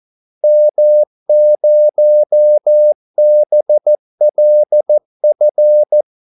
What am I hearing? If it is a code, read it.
Morse code, heard as a keyed tone, about 14 wpm, 600 Hz. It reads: M0BLF
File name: Callsigns-14wpm